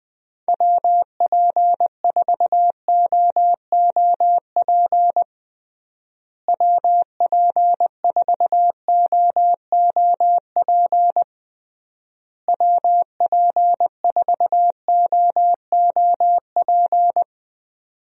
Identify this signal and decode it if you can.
Morse code, heard as a keyed tone, about 21 wpm, 700 Hz. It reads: WP4OOP WP4OOP WP4OOP
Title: USA Call Signs - 6 Characters - Repeated 3x - 20wpm